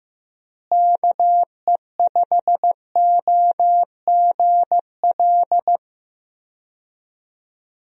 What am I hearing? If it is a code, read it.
Morse code, heard as a keyed tone, about 16 wpm, 700 Hz. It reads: KE5OGL
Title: USA Call Signs - 6 Characters - Repeated 2x - 15wpm